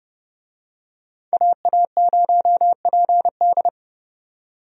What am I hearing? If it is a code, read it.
Morse code, heard as a keyed tone, about 30 wpm, 700 Hz. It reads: AA0PD
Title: USA Call Signs - 5 Characters - 30wpm